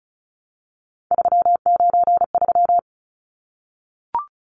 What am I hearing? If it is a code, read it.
Morse code, heard as a keyed tone, about 34 wpm, 700 Hz. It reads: 393
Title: Numbers - 3-Digit 35wpm